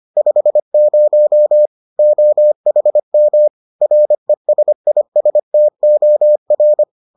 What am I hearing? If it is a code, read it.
Morse code, heard as a keyed tone, about 25 wpm, 600 Hz. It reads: 50 OHM RESISTOR